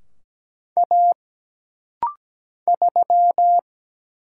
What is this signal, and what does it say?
Morse code, heard as a keyed tone, about 17 wpm, 700 Hz. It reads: A 3
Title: Single Letter-Number - Rapid-Fire 17wpm